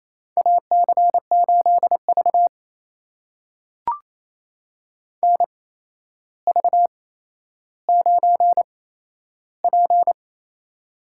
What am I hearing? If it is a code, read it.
Morse code, heard as a keyed tone, about 27 wpm, spaced as for 12 wpm, 700 Hz. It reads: AC8V NV9P
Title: USA Call Signs - 4 Characters - 8x Character Spacing - 28wpm